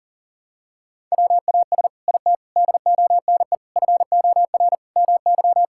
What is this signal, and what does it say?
Morse code, heard as a keyed tone, about 40 wpm, 700 Hz. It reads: WAS IT DONE FOR MY
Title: Sentences from Top 1000 Words - Review Part 6 - 40wpm